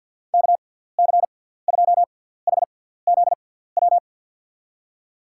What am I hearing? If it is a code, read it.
Morse code, heard as a keyed tone, about 47 wpm, spaced as for 17 wpm, 700 Hz. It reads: KC2HZW
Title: USA Call Signs - 6 Characters - 6x Character Spacing - 50wpm